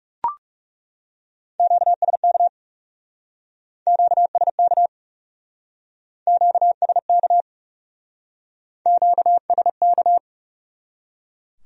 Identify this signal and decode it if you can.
Morse code, heard as a keyed tone, about 39 wpm, 700 Hz. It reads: QSK QSK QSK QSK